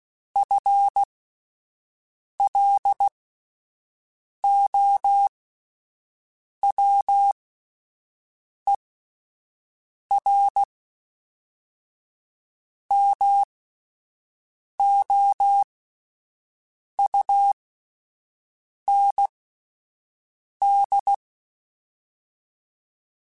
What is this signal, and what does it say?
Morse code, heard as a keyed tone, about 16 wpm, spaced as for 5 wpm, 800 Hz. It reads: FLOWERMOUND
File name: fromSMs / FromSMs mp3s / Cities@5